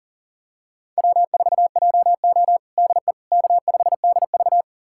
Morse code, heard as a keyed tone, about 40 wpm, 700 Hz. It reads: W4JO DE K5DV
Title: USA Callsign DE Callsign - 4 Characters - 40wpm